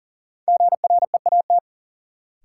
Morse code, heard as a keyed tone, about 40 wpm, 700 Hz. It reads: GREAT
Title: Sets of 2 Words - New Word Drill - 40wpm